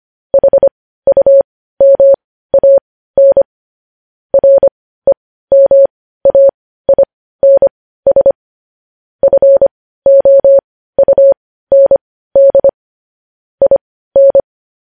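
Morse code, heard as a keyed tone, about 25 wpm, spaced as for 15 wpm, 550 Hz. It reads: HUMAN REMAINS FOUND IN
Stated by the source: News headlines sent in Morse code at 15 wpm.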